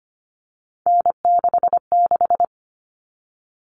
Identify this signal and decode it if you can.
Morse code, heard as a keyed tone, about 25 wpm, 700 Hz. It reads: N66
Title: All Cut Numbers - 3-Digit 25wpm